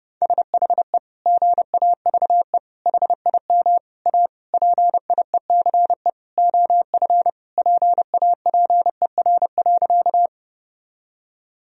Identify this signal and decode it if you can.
Morse code, heard as a keyed tone, about 30 wpm, 700 Hz. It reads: SHE GAVE HIM A PIECE OF PAPER.
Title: Sentences from Top 1000 Words - Review Part 2 - 30wpm